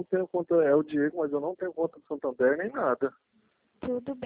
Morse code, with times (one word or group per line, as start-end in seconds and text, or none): none